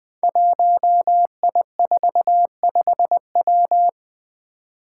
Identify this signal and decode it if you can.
Morse code, heard as a keyed tone, about 20 wpm, 700 Hz. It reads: 1I45W
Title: Five Challenging Letter-Number Combinations 20wpm